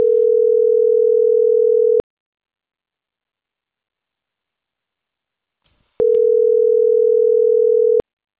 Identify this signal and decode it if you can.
Ringback tone (North America), 440 + 480 Hz.